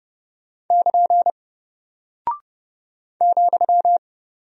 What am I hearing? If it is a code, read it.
Morse code, heard as a keyed tone, about 30 wpm, 700 Hz. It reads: <KN> ,
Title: Single Character (Letter:Number:Punctuation:Prosign) Alt 30wpm